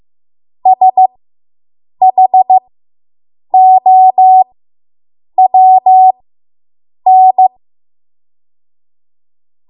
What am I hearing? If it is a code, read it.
Morse code, heard as a keyed tone, about 15 wpm, spaced as for 7 wpm, 750 Hz. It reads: SHOWN